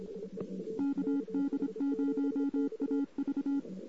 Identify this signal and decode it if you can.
Morse code, heard as a keyed tone, about 26 wpm, 280 Hz. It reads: KD0AV
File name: CW / 2008-ss-cw-k5tr-op-n5rz -14:00